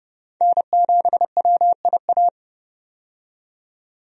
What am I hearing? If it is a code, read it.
Morse code, heard as a keyed tone, about 29 wpm, 700 Hz. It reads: N7WIA
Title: USA Call Signs - 5 Characters - 30wpm